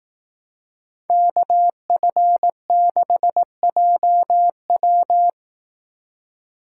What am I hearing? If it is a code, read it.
Morse code, heard as a keyed tone, about 18 wpm, 700 Hz. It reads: KF6JW
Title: Speed Racing - USA 2x2 Call Signs 15wpm